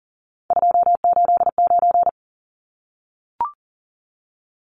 Morse code, heard as a keyed tone, about 40 wpm, 700 Hz. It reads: 289
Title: Numbers - 3-Digit 40wpm